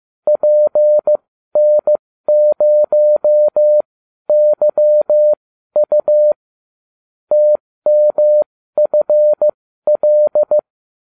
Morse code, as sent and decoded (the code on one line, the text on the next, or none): .--. -. ----- -.-- ..- / - -- ..-. .-..
PN0YU TMFL